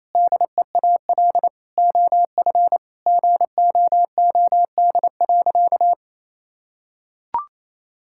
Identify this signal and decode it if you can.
Morse code, heard as a keyed tone, about 28 wpm, 700 Hz. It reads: DEAL OF GOOD.
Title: Sentences from Top 1000 Words - Review Part 6 - 28wpm